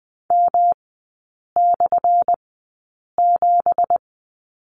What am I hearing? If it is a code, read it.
Morse code, heard as a keyed tone, about 20 wpm, spaced as for 8 wpm, 700 Hz. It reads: M/7
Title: 178 Lesson 29e - Slash - Three Characters 20wpm